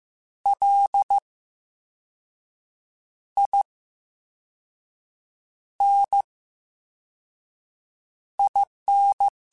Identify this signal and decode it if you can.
Morse code, heard as a keyed tone, about 15 wpm, spaced as for 5 wpm, 800 Hz. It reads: LININ
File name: Special INGs-mp3s / Special INGs-mp3s / VerySlowToFaster Ing @13